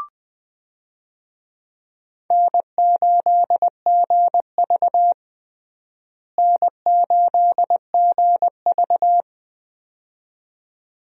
Morse code, heard as a keyed tone, about 20 wpm, 700 Hz. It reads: N8GV N8GV
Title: USA Call Signs - 4 Characters - Repeated 2x - 20wpm